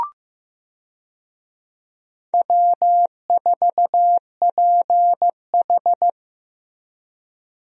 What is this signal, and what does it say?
Morse code, heard as a keyed tone, about 15 wpm, 700 Hz. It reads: W4PH
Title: USA Call Signs - 4 Characters - Repeated 2x - 15wpm